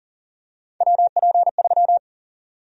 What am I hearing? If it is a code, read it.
Morse code, heard as a keyed tone, about 40 wpm, 700 Hz. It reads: WP3
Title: USA Call Sign Prefixes - 40wpm